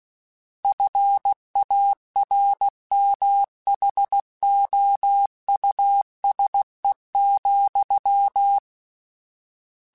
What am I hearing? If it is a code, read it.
Morse code, heard as a keyed tone, about 16 wpm, 800 Hz. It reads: FARMHOUSE,
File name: General / General-mp3s / Houses @15